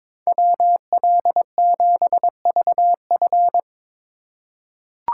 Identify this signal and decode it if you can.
Morse code, heard as a keyed tone, about 23 wpm, 700 Hz. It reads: WL7VF
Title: USA Call Signs - 5 Characters - 2x Character Spacing - 22wpm